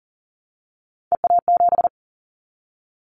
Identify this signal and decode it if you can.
Morse code, heard as a keyed tone, about 40 wpm, 700 Hz. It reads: EA7
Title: Callsign Prefixes for Top 25 DXCC - Repeated 3x - 40wpm